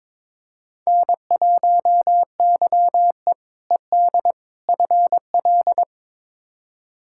Morse code, heard as a keyed tone, about 22 wpm, 700 Hz. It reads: N1YE ED FL
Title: K1USN SST QSO Practice - 22wpm